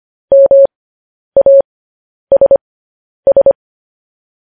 Morse code, heard as a keyed tone, about 25 wpm, spaced as for 10 wpm, 550 Hz. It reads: MASS